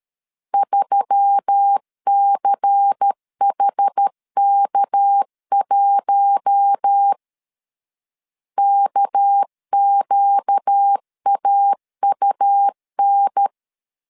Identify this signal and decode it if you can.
Morse code, heard as a keyed tone, about 13 wpm, 800 Hz. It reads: 3CHK1 KQAUN
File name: LEZIONI / 6 PKX0 / 6 PK 12wpm 2006